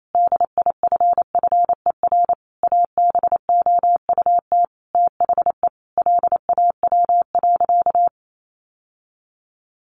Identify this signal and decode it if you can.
Morse code, heard as a keyed tone, about 28 wpm, 700 Hz. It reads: DIFFER ABOUT THE LAW.
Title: Sentences from Top 1400 Words 28wpm